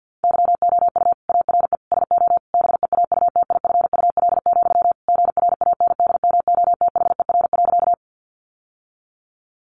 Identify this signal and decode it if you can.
Morse code, heard as a keyed tone, about 50 wpm, 700 Hz. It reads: YOU ARE SO BEAUTIFUL, GRANDMOTHER.
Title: Sentences from Top 1200 Words 50wpm